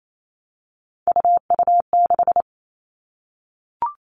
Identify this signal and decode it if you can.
Morse code, heard as a keyed tone, about 28 wpm, 700 Hz. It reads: UU6
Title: All Cut Numbers - 3-Digit 28wpm